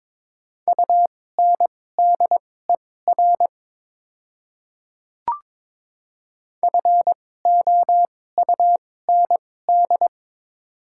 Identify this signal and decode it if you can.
Morse code, heard as a keyed tone, about 21 wpm, spaced as for 16 wpm, 700 Hz. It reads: UNDER FOUND
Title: 2x Character Spacing - Top 300 Words - 22wpm